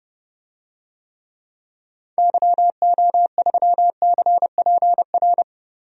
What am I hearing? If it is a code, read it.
Morse code, heard as a keyed tone, about 30 wpm, 700 Hz. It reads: YO3CPR
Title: DXCC-100 Call Signs - 6 Characters - Repeated 2x - 30wpm